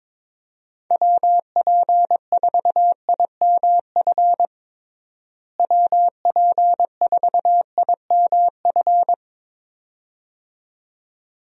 Morse code, heard as a keyed tone, about 22 wpm, 700 Hz. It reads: WP4IMF WP4IMF
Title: USA Call Signs - 6 Characters - Repeated 2x - 22wpm